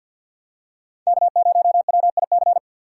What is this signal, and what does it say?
Morse code, heard as a keyed tone, about 52 wpm, 700 Hz. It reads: K0WIC
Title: USA Call Signs - 5 Characters - 2x Character Spacing - 50wpm